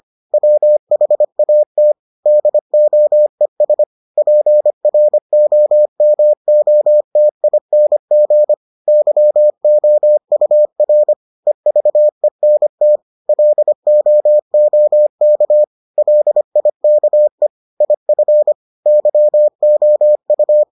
Morse code, heard as a keyed tone, about 25 wpm, 600 Hz. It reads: WHAT DOES PROMOTING YOUR EVENT LOOK LIKE IF YOU